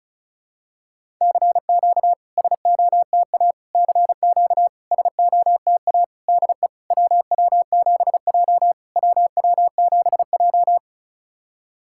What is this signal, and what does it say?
Morse code, heard as a keyed tone, about 36 wpm, 700 Hz. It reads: CQ SOTA CQ SOTA DE WW7J WW7J
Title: SOTA QSO Practice 35wpm